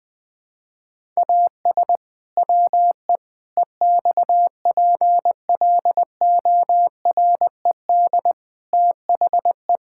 Morse code, heard as a keyed tone, about 20 wpm, 700 Hz. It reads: AS WE EXPLORED THE
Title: Pangram Challenge v2 - 20wpm